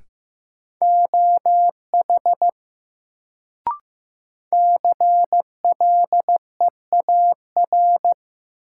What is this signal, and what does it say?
Morse code, heard as a keyed tone, about 16 wpm, 700 Hz. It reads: OH CLEAR
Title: Sentences from Top 500 Words - New Word Drill - 15wpm